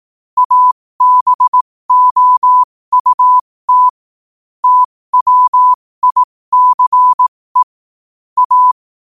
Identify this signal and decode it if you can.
Morse code, heard as a keyed tone, about 18 wpm, spaced as for 15 wpm, 1000 Hz. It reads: ABOUT TWICE A
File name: qotdfiles / 15wpm